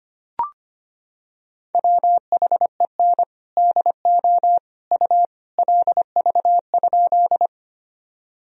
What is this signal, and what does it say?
Morse code, heard as a keyed tone, about 25 wpm, 700 Hz. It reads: WHEN DO U LV?
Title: Sentences from Top 400 Words and Abbreviations 25wpm